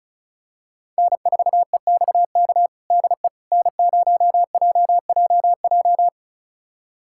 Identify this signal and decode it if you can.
Morse code, heard as a keyed tone, about 36 wpm, 700 Hz. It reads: N4EXK DE N0JJJ
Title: USA Callsign DE Callsign - 5 Characters - 35wpm